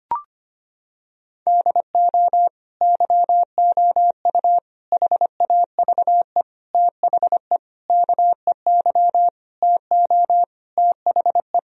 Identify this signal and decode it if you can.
Morse code, heard as a keyed tone, about 25 wpm, 700 Hz. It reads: DO YOU HAVE THE KEY TO THE
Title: Sentences from Top 1000 Words - Review Part 4 - 25wpm